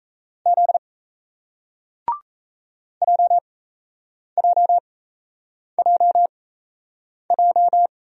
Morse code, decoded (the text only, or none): Z JJJJ